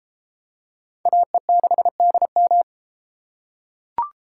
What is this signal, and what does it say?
Morse code, heard as a keyed tone, about 32 wpm, 700 Hz. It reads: AE6DM